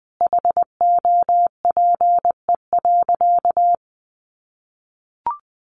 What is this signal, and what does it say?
Morse code, heard as a keyed tone, about 20 wpm, 700 Hz. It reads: HOPE.
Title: Sentences from Top 1200 Words 20wpm